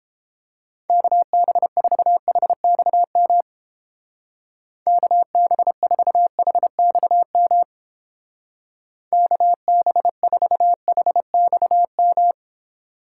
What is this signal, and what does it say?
Morse code, heard as a keyed tone, about 31 wpm, 700 Hz. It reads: KB4HXM KB4HXM KB4HXM